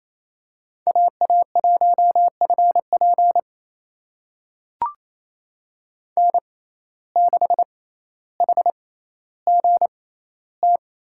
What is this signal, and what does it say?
Morse code, heard as a keyed tone, about 28 wpm, spaced as for 14 wpm, 700 Hz. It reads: AA1FP N6HGT
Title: USA Call Signs - 5 Characters - 6x Character Spacing - 28wpm